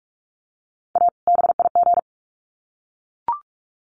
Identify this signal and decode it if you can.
Morse code, heard as a keyed tone, about 44 wpm, 700 Hz. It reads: A BIG
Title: Sets of 2 Words 45wpm